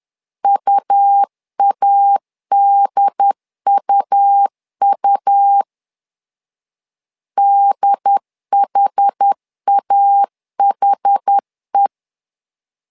Morse code, heard as a keyed tone, about 10 wpm, 800 Hz. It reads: UADUU DHAHE
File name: LEZIONI / 2 UHDN / 2 UHD 10wpm 1826